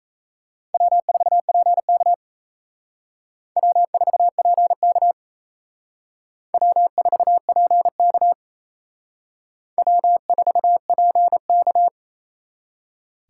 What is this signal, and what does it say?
Morse code, heard as a keyed tone, about 34 wpm, 700 Hz. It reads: W4PK W4PK W4PK W4PK
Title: Speed Racing - USA 1x2 Call Signs 28wpm